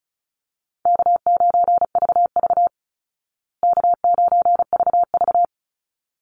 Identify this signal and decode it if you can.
Morse code, heard as a keyed tone, about 34 wpm, 700 Hz. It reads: K9VV K9VV
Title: Call Signs Repeated 3x - 35wpm